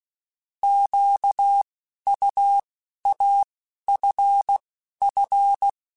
Morse code, heard as a keyed tone, about 16 wpm, 800 Hz. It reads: Q U A F F
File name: Alphabetical / Alphabetical mp3s / Qs @10